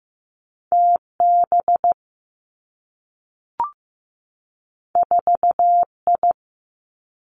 Morse code, heard as a keyed tone, about 15 wpm, 700 Hz. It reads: T B 4 I
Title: Two Challenging Letter-Number Combinations 15wpm